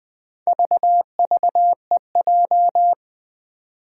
Morse code, heard as a keyed tone, about 20 wpm, 700 Hz. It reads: VVEJ
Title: Four Challenging Letter-Number Combinations 20wpm